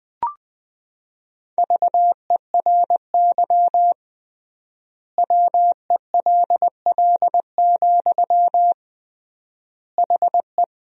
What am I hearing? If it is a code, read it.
Morse code, heard as a keyed tone, about 20 wpm, 700 Hz. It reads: VERY WELL, HE
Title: 3x Word-Spacing - Sentences from Top 200 Words - 20wpm